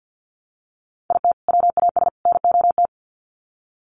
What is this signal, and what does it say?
Morse code, heard as a keyed tone, about 50 wpm, 700 Hz. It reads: IT WAS NOT
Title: Sets of 3 Words 50wpm